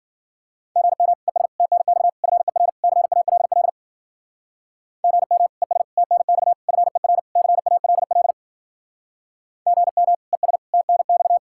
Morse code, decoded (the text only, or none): GM ES TNX FER CALL GM ES TNX FER CALL GM ES TNX